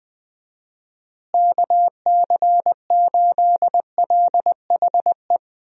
Morse code, heard as a keyed tone, about 19 wpm, 700 Hz. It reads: KC8LHE
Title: USA Call Signs - 6 Characters - Repeated 2x - 20wpm